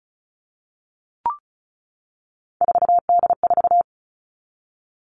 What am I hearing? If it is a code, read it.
Morse code, heard as a keyed tone, about 34 wpm, 700 Hz. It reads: 4D4
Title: All Cut Numbers - 3-Digit 35wpm